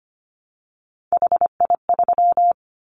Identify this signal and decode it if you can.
Morse code, heard as a keyed tone, about 25 wpm, 700 Hz. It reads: HI3
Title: Callsign Prefixes for Top 100 DXCC - 25wpm